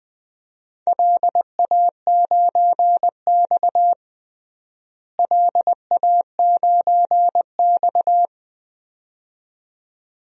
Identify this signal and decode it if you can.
Morse code, heard as a keyed tone, about 20 wpm, 700 Hz. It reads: LA9X LA9X
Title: DXCC-100 Call Signs - 4 Characters - Repeated 2x - 20wpm